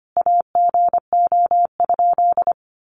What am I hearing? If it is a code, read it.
Morse code, heard as a keyed tone, about 25 wpm, 700 Hz. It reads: AGO?
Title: Sentences from Top 500 Words 25wpm